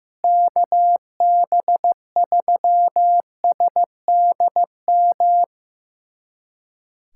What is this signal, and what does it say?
Morse code, heard as a keyed tone, about 16 wpm, 700 Hz. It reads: KB3SDM
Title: USA Callsign DE Callsign - 6 Characters - 15wpm